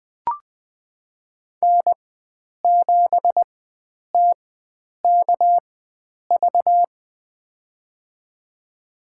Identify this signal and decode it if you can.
Morse code, heard as a keyed tone, about 20 wpm, spaced as for 9 wpm, 700 Hz. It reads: N7TKV
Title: USA Call Signs - 4x Character Spacing - 20wpm